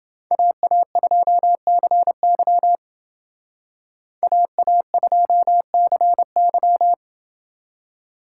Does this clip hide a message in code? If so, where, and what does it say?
Yes: Morse code from 0.3 to 6.9 s, heard as a keyed tone, about 28 wpm, 700 Hz. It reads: AA2CY AA2CY